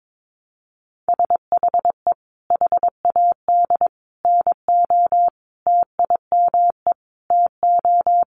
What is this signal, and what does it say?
Morse code, heard as a keyed tone, about 22 wpm, 700 Hz. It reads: SHE HAD NO TIME TO
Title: Sentences from Top 1100 Words 22wpm